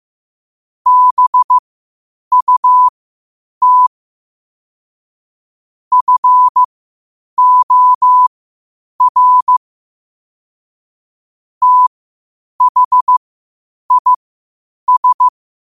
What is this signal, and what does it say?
Morse code, heard as a keyed tone, about 15 wpm, spaced as for 8 wpm, 1000 Hz. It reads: BUT FOR THIS
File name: qotdfiles / 8wpm